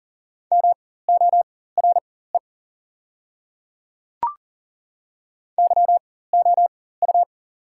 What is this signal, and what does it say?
Morse code, heard as a keyed tone, about 39 wpm, spaced as for 18 wpm, 700 Hz. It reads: MORE YOU
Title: 4x Character Spacing - Top 100 Words - 40wpm